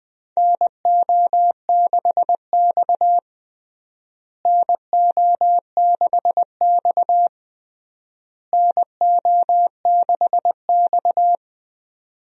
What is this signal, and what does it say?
Morse code, heard as a keyed tone, about 21 wpm, 700 Hz. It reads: NO6X NO6X NO6X
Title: USA Call Signs - 4 Characters - Repeated 3x - 20wpm